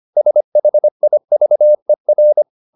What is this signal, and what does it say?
Morse code, heard as a keyed tone, about 25 wpm, 600 Hz. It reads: SHIVER
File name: the old man and the sea part 1 25wpm